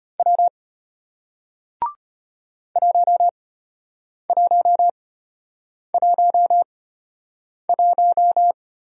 Morse code, heard as a keyed tone, about 33 wpm, spaced as for 8 wpm, 700 Hz. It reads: W 1111